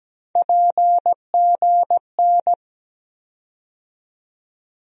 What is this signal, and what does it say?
Morse code, heard as a keyed tone, about 17 wpm, 700 Hz. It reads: PGN
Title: USA Call Sign Postfixes - 3 Letters 17wpm